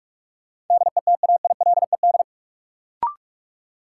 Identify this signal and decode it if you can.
Morse code, heard as a keyed tone, about 44 wpm, 700 Hz. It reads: DETAILED
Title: Sentences from Top 1500 Words - New Word Drill Alt - 45wpm